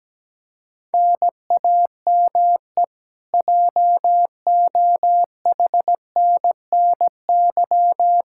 Morse code, heard as a keyed tone, about 17 wpm, 700 Hz. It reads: NAME JOHNNY
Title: Practice QSO - Exchange 1 - Beginning - 17wpm